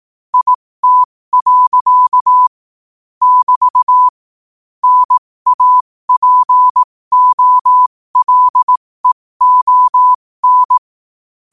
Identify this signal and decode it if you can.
Morse code, heard as a keyed tone, about 18 wpm, spaced as for 15 wpm, 1000 Hz. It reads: IT. <BT> NAPOLEON
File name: qotdfiles / 15wpm